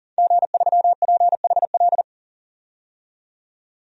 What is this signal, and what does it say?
Morse code, heard as a keyed tone, about 40 wpm, 700 Hz. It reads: G3PHL
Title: DXCC-100 Call Signs - 5 Characters - 40wpm